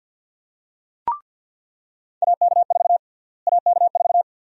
Morse code, heard as a keyed tone, about 48 wpm, 700 Hz. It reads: AK4 AK4
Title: USA Call Sign Prefixes - Repeated 2x - 50wpm